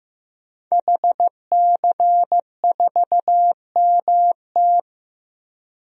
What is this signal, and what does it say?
Morse code, heard as a keyed tone, about 15 wpm, 700 Hz. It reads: HC4MT
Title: DXCC-100 Call Signs - 5 Characters - 15wpm